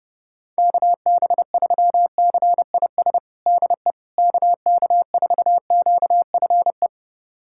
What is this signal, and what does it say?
Morse code, heard as a keyed tone, about 29 wpm, 700 Hz. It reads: KB3CIS DE KK4QFE
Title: USA Callsign DE Callsign - 6 Characters - 30wpm